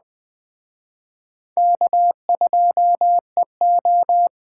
Morse code, heard as a keyed tone, about 20 wpm, 700 Hz. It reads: K2EO
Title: Speed Racing - USA 1x2 Call Signs 17wpm